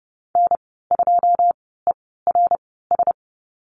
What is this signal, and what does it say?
Morse code, heard as a keyed tone, about 30 wpm, spaced as for 17 wpm, 700 Hz. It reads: N2ERS
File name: Lesson 15g - Number 5 - Call Signs 30wpm